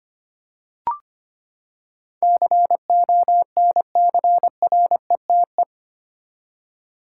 Morse code, heard as a keyed tone, about 25 wpm, 700 Hz. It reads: CONCRETE
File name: Eight Letter Words - 25wpm